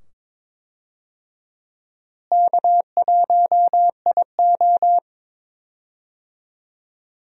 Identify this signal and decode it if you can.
Morse code, heard as a keyed tone, about 22 wpm, 700 Hz. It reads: K1IO
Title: USA Call Signs - 4 Characters - Repeated 2x - 22wpm